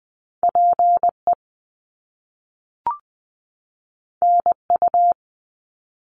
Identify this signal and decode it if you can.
Morse code, heard as a keyed tone, about 20 wpm, 700 Hz. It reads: PE NU
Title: Canadian Provinces and Territories 20wpm